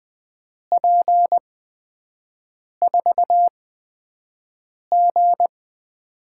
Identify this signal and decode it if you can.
Morse code, heard as a keyed tone, about 20 wpm, spaced as for 5 wpm, 700 Hz. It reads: P4G
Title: USA Call Signs - 5 Characters - 8x Character Spacing - 20wpm